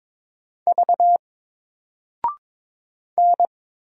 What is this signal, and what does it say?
Morse code, heard as a keyed tone, about 22 wpm, 700 Hz. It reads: V N
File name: Single-Letter-Number-22wpm